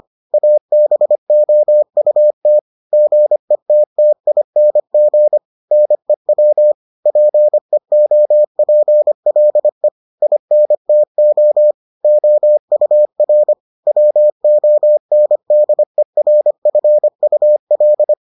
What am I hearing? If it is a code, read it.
Morse code, heard as a keyed tone, about 25 wpm, 600 Hz. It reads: ABOUT GETTING NEW PEOPLE INTO OUR WONDERFUL